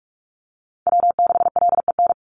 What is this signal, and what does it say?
Morse code, heard as a keyed tone, about 46 wpm, 700 Hz. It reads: W6LEN
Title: Call Signs - 45wpm